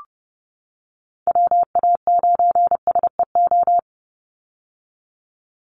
Morse code, heard as a keyed tone, about 30 wpm, 700 Hz. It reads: WA9SEO
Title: Call Signs - 30wpm